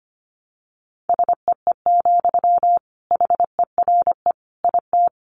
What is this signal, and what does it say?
Morse code, heard as a keyed tone, about 25 wpm, 700 Hz. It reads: SEE, HERE IT
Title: Sentences from Top 200 Words 25wpm